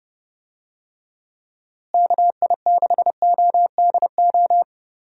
Morse code, heard as a keyed tone, about 31 wpm, 700 Hz. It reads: KI6ODO
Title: USA Call Signs - 6 Characters - Repeated 3x - 30wpm